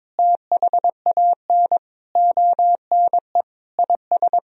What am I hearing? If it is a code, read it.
Morse code, heard as a keyed tone, about 21 wpm, 700 Hz. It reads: THAN ONE IS
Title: Sentences from Top 1000 Words - Review Part 2 - 22wpm